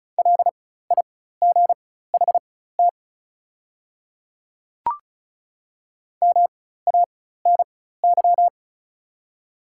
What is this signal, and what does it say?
Morse code, heard as a keyed tone, about 36 wpm, spaced as for 16 wpm, 690 Hz. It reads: LIGHT MANY
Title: 4x Character Spacing - Top 200 Words - 35wpm